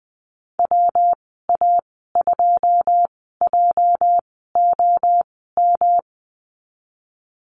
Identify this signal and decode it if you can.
Morse code, heard as a keyed tone, about 20 wpm, spaced as for 14 wpm, 700 Hz. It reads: WA2JOM
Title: 245 Lesson 39g - Letter z - Call Signs 20wpm